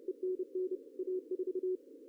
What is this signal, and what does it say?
Morse code, heard as a keyed tone, about 30 wpm, 340 Hz, with radio noise. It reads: ENN A4
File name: CW / cqww-cw-k5tr